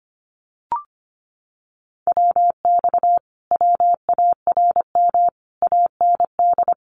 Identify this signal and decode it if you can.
Morse code, heard as a keyed tone, about 25 wpm, 700 Hz. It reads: WX WARM AND
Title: Common QSO Phrases Encore 25wpm